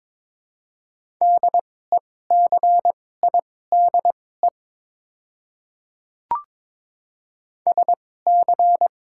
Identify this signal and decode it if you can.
Morse code, heard as a keyed tone, about 23 wpm, spaced as for 16 wpm, 700 Hz. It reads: DECIDE SC
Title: 2x Character Spacing - Top 500 Words - 22wpm